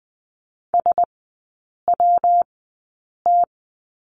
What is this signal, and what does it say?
Morse code, heard as a keyed tone, about 20 wpm, spaced as for 8 wpm, 700 Hz. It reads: SWT
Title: 114 Lesson 19e - Letter w - Three Characters 20wpm